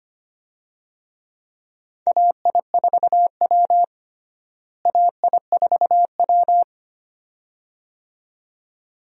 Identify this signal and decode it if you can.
Morse code, heard as a keyed tone, about 25 wpm, 700 Hz. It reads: AI4W AI4W
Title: USA Call Signs - 4 Characters - Repeated 2x - 25wpm